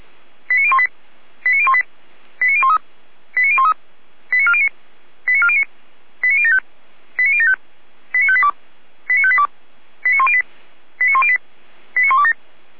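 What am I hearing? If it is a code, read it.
Five-tone selective call (ZVEI), heard as quick runs of five tones, tones of about 70 ms. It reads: 89018 89018 89013 89013 89409 89409 89075 89075 89582 89582 89108 89108 89127
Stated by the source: ZVEI five-tone selective call, code 89560, received via SDR.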